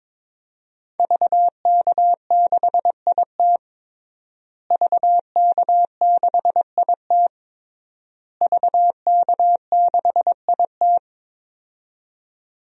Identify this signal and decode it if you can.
Morse code, heard as a keyed tone, about 22 wpm, 700 Hz. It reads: VK6IT VK6IT VK6IT
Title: DXCC-100 Call Signs - 5 Characters - Repeated 3x - 22wpm